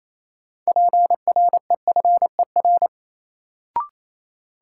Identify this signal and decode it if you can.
Morse code, heard as a keyed tone, about 28 wpm, 700 Hz. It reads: PREFER
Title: Sentences from Top 1100 Words - New Word Drill - 28wpm